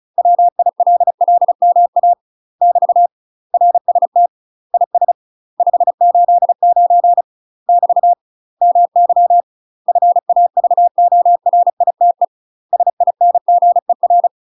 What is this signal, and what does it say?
Morse code, heard as a keyed tone, about 36 wpm, 700 Hz. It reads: WILLMA <BT> RST IS 589 <BT> MY FAVORITE SINGER
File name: UK MASTER QSOs TRAINING / 35WPM / UK MASTER QSOs 0013